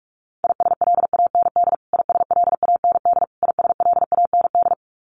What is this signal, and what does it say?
Morse code, heard as a keyed tone, about 44 wpm, 700 Hz. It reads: ISLAND ISLAND ISLAND
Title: Top 500 Words Repeated 45wpm